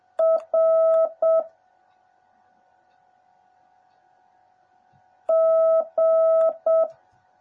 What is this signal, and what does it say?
Morse code, heard as a keyed tone, about 7 wpm, 650 Hz. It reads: R G
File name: jh8cbh / 2021cw / cw09a